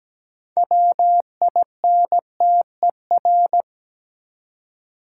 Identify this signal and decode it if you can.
Morse code, heard as a keyed tone, about 17 wpm, 700 Hz. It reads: WINTER
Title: Six Letter Words - 17wpm